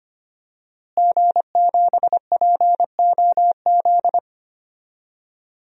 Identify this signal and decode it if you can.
Morse code, heard as a keyed tone, about 25 wpm, 700 Hz. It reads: G7POZ
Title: DXCC-100 Call Signs - 5 Characters - Repeated 3x - 25wpm